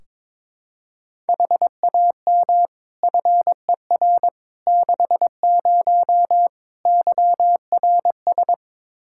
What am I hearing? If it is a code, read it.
Morse code, heard as a keyed tone, about 22 wpm, 700 Hz. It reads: HAM FER 60 YRS
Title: Practice QSO - Exchange 3 - More Info - 22wpm